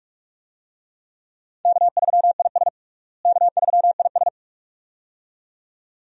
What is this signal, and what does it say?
Morse code, heard as a keyed tone, about 45 wpm, 700 Hz. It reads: K3IS K3IS
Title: USA Call Signs - 4 Characters - Repeated 2x - 45wpm